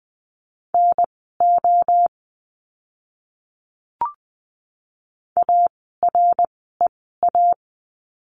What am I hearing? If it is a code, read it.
Morse code, heard as a keyed tone, about 20 wpm, spaced as for 14 wpm, 700 Hz. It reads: NO AREA
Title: 061 Lesson 11f - Letter h - Words 20wpm